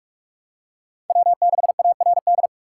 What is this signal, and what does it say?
Morse code, heard as a keyed tone, about 44 wpm, 700 Hz. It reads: W6ARD